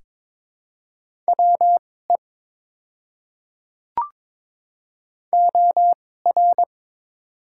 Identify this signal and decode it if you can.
Morse code, heard as a keyed tone, about 22 wpm, spaced as for 16 wpm, 690 Hz. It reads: WE OR
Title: 2x Character Spacing - Top 300 Words - 22wpm